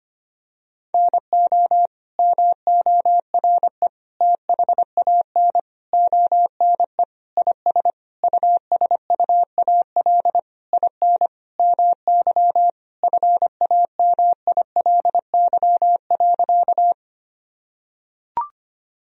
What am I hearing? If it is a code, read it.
Morse code, heard as a keyed tone, about 25 wpm, 700 Hz. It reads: NO MORE THAN ONE IS USUAL IN MY FAMILY.
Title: Sentences from Top 1000 Words - Review Part 2 - 25wpm